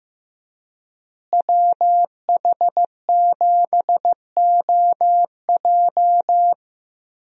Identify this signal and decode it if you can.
Morse code, heard as a keyed tone, about 15 wpm, 700 Hz. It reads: WH7OJ
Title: USA Call Signs - 5 Characters - Repeated 3x - 15wpm